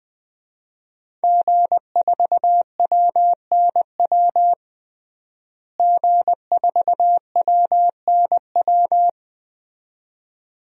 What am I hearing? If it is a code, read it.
Morse code, heard as a keyed tone, about 21 wpm, 700 Hz. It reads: G4WNW G4WNW
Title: DXCC-100 Call Signs - 5 Characters - Repeated 2x - 20wpm